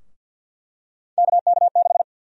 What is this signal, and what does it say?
Morse code, heard as a keyed tone, about 51 wpm, 700 Hz. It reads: KK6
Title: USA Call Sign Prefixes - 50wpm